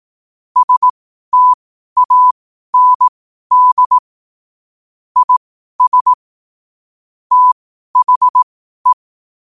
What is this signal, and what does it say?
Morse code, heard as a keyed tone, about 18 wpm, spaced as for 12 wpm, 1000 Hz. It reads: STAND IS THE
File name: qotdfiles / 12wpm